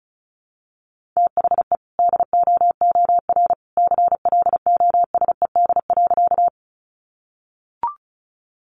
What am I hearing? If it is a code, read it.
Morse code, heard as a keyed tone, about 34 wpm, 700 Hz. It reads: THE DOOR CLOSED.
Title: Sentences from Top 1400 Words 35wpm